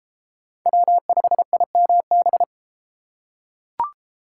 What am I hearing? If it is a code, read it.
Morse code, heard as a keyed tone, about 31 wpm, 700 Hz. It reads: W5IMB